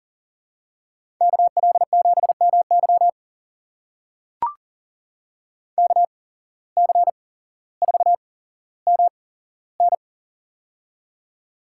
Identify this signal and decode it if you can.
Morse code, heard as a keyed tone, about 39 wpm, spaced as for 17 wpm, 700 Hz. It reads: KL7MY KC4MN
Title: USA Call Signs - 5 Characters - 8x Character Spacing - 40wpm